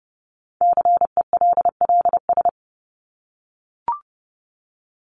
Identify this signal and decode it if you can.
Morse code, heard as a keyed tone, about 30 wpm, 700 Hz. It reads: CELLS
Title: ES:S Suffix - Top 100 Words with Base Words 30wpm